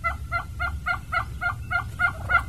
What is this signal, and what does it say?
There is no code.